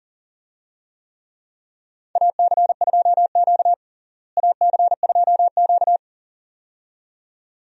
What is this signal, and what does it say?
Morse code, heard as a keyed tone, about 40 wpm, 700 Hz. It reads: AC2Q AC2Q
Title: USA Call Signs - 4 Characters - Repeated 2x - 40wpm